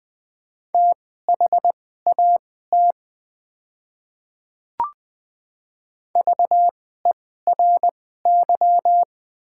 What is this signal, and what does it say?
Morse code, heard as a keyed tone, about 20 wpm, spaced as for 15 wpm, 700 Hz. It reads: THAT VERY
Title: 2x Character Spacing - Top 200 Words - 20wpm